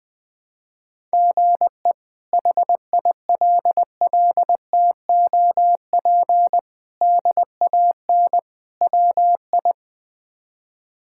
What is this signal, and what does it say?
Morse code, heard as a keyed tone, about 21 wpm, 700 Hz. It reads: GE HILLTOP DAN WI
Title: K1USN SST QSO Practice - 20wpm